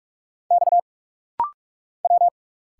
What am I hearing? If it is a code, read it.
Morse code, heard as a keyed tone, about 46 wpm, 700 Hz. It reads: X W
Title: Single Letters - Rapid-Fire 45wpm